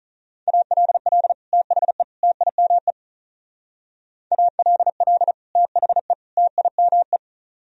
Morse code, heard as a keyed tone, about 44 wpm, 700 Hz. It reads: ALL THE TIME ALL THE TIME